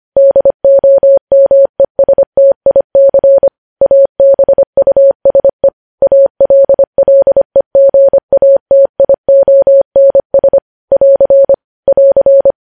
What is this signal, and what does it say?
Morse code, heard as a keyed tone, about 25 wpm, 550 Hz. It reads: DOMESTIC ABUSE ALLEGATIONS <AR> <AR>
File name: podcast / News headlines-1x-25x25